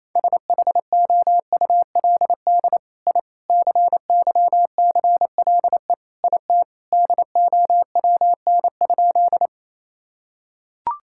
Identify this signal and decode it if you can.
Morse code, heard as a keyed tone, about 28 wpm, 700 Hz. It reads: SHOULD I CYCLE IT DOWN?
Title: Sentences from Top 1000 Words - Review Part 6 - 28wpm